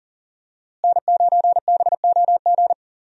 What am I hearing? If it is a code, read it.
Morse code, heard as a keyed tone, about 40 wpm, 700 Hz. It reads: N9BOG
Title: USA Call Signs - 5 Characters - Repeated 3x - 40wpm